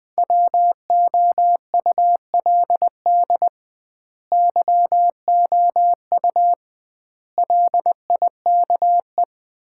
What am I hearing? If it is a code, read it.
Morse code, heard as a keyed tone, about 20 wpm, 700 Hz. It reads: WOULD YOU LIKE
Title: 2x Word-Spacing - Sets of 4 Words - 20wpm